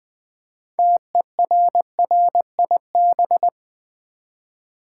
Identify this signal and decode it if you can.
Morse code, heard as a keyed tone, about 20 wpm, 700 Hz. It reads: TERRIB
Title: Build A Word - 3 Letters At A Time - 8-Letter Words - 20wpm